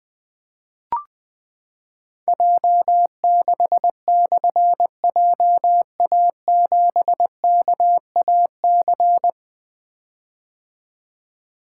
Morse code, heard as a keyed tone, about 20 wpm, 700 Hz. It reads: J6/JA7KAC
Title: Difficult Call Signs 20wpm